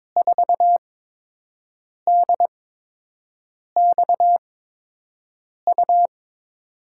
Morse code, heard as a keyed tone, about 22 wpm, spaced as for 6 wpm, 700 Hz. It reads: 4DXU
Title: USA Call Signs - 5 Characters - 8x Character Spacing - 22wpm